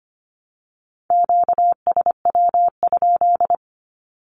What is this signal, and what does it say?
Morse code, heard as a keyed tone, about 25 wpm, 700 Hz. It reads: QSW?
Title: Q-Codes Repeated 3x 25wpm